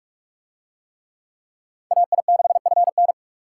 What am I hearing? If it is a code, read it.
Morse code, heard as a keyed tone, about 45 wpm, 700 Hz. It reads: AI6FN